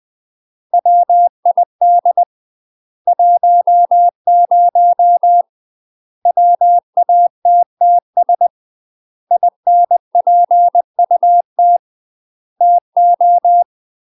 Morse code, heard as a keyed tone, about 20 wpm, 700 Hz. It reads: WID 10 WATTS INPUT TO
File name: UK MASTER QSOs TRAINING/20WPM